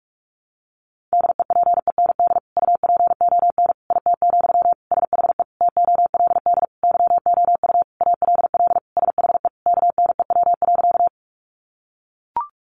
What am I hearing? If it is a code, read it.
Morse code, heard as a keyed tone, about 44 wpm, 700 Hz. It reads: DEPEND UPON IT, SHE TOLD YOU ALL SHE KNEW.
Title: Sentences from Top 1300 Words 45wpm